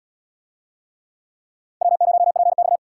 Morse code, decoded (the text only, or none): W0PX